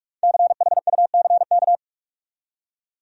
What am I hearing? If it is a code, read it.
Morse code, heard as a keyed tone, about 46 wpm, 700 Hz. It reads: CHUCK
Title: USA Common Names - 45wpm